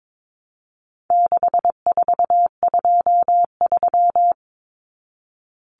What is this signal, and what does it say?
Morse code, heard as a keyed tone, about 22 wpm, 700 Hz. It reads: 6423
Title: Numbers - 4-Digit 22wpm